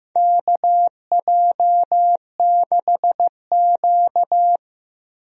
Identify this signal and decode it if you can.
Morse code, heard as a keyed tone, about 15 wpm, 700 Hz. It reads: KJ6Q
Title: USA Call Signs - 4 Characters - Repeated 3x - 15wpm